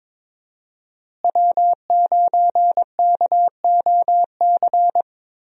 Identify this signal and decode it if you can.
Morse code, heard as a keyed tone, about 22 wpm, 700 Hz. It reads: W9KOC
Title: USA Call Signs - 5 Characters - Repeated 3x - 22wpm